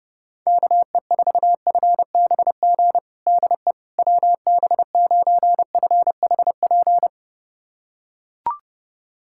Morse code, heard as a keyed tone, about 32 wpm, 700 Hz. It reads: KE4FBG DE WB9FHP
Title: USA Callsign DE Callsign - 6 Characters - 30wpm